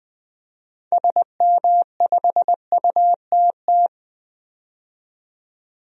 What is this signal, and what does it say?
Morse code, heard as a keyed tone, about 20 wpm, 700 Hz. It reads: SM5UTT
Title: DXCC-100 Call Signs - 6 Characters - 20wpm